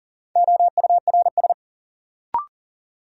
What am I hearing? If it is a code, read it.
Morse code, heard as a keyed tone, about 37 wpm, 700 Hz. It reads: OURS
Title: Sentences from Top 1300 Words - New Word Drill Alt - 40wpm